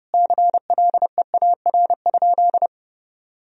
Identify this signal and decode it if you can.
Morse code, heard as a keyed tone, about 30 wpm, 700 Hz. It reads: CLEAR?
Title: Sentences from Top 1000 Words - Review Part 4 - 30wpm